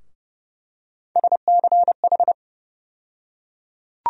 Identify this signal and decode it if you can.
Morse code, heard as a keyed tone, about 29 wpm, 700 Hz. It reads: SCH
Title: USA Call Sign Postfixes - 3 Letters 30wpm